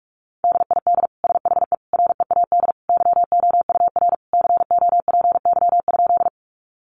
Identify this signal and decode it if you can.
Morse code, heard as a keyed tone, about 44 wpm, 700 Hz. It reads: DID SHE READ YOUR COPY?
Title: Sentences from Top 1400 Words 45wpm